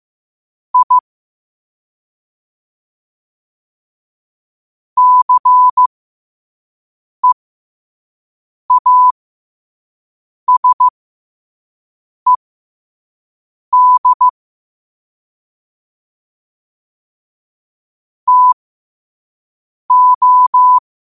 Morse code, heard as a keyed tone, about 15 wpm, spaced as for 5 wpm, 1000 Hz. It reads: I CEASED TO
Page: New quotes every day in morse code at 5 Words per minute.